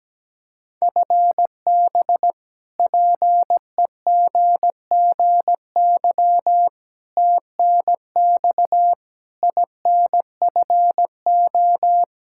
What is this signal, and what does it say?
Morse code, heard as a keyed tone, about 17 wpm, 700 Hz. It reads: FB PEGGY TNX INFO
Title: Practice QSO - Exchange 3 - More Info - 17wpm